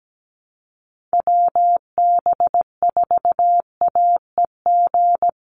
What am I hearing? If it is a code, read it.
Morse code, heard as a keyed tone, about 17 wpm, 700 Hz. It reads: WB4AEG
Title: Call Signs - 17wpm